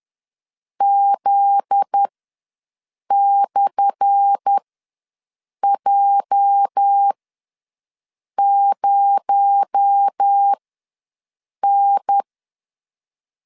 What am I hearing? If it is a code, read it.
Morse code, heard as a keyed tone, about 10 wpm, spaced as for 6 wpm, 800 Hz. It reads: Z/J0N
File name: LEZIONI / 10 SIGNS / 10A SLASH ws 10wpm 4121